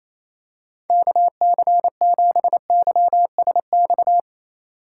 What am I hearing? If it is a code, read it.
Morse code, heard as a keyed tone, about 28 wpm, 700 Hz. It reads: KC7YSX
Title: USA Call Signs - 2x Character Spacing - 28wpm